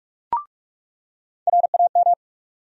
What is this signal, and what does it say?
Morse code, heard as a keyed tone, about 45 wpm, 700 Hz. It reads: RAM
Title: USA Call Sign Postfixes - 3 Letters 45wpm